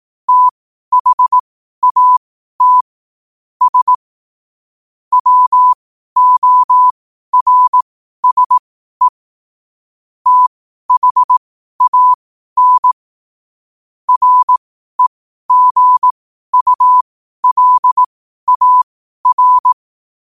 Morse code, heard as a keyed tone, about 18 wpm, spaced as for 12 wpm, 1000 Hz. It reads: THAT S WORSE THAN REGULAR